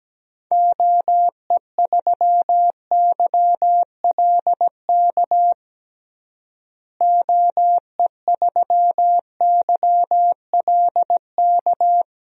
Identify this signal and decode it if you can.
Morse code, heard as a keyed tone, about 17 wpm, 700 Hz. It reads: OE3YLK OE3YLK
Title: DXCC-100 Call Signs - 6 Characters - Repeated 3x - 17wpm